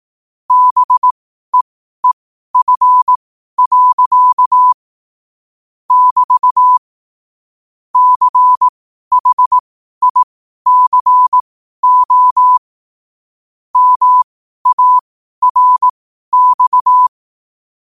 Morse code, heard as a keyed tone, about 18 wpm, spaced as for 12 wpm, 1000 Hz. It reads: BEEF. <BT> CHICO MARX
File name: qotdfiles / 12wpm